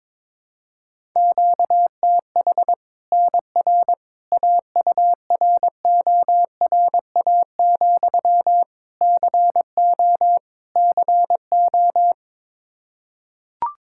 Morse code, heard as a keyed tone, about 22 wpm, 700 Hz. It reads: QTH NR AURORA, CO CO
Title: Practice QSO - Exchange 1 - Beginning - 22wpm